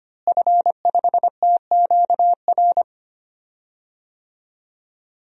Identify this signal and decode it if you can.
Morse code, heard as a keyed tone, about 25 wpm, 700 Hz. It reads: F5TQR
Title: DXCC-100 Call Signs - 5 Characters - Repeated 3x - 25wpm